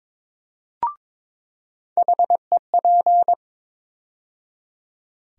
Morse code, heard as a keyed tone, about 22 wpm, 700 Hz. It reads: HEP
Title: USA Call Sign Postfixes - 3 Letters 22wpm